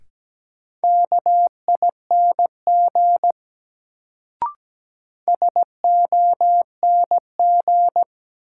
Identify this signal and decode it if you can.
Morse code, heard as a keyed tone, about 17 wpm, 700 Hz. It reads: KING SONG
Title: Sentences from Top 400 Words - New Word Drill - 17wpm